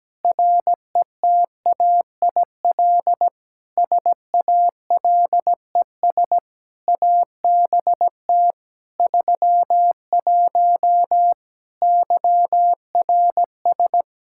Morse code, heard as a keyed tone, about 17 wpm, 700 Hz. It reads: RETAIL SALES ABT 31 YRS
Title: Practice QSO - Exchange 3 - More Info - 17wpm